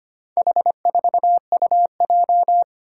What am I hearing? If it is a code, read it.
Morse code, heard as a keyed tone, about 25 wpm, 700 Hz. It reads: H4UJ
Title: Four Challenging Letter-Number Combinations 25wpm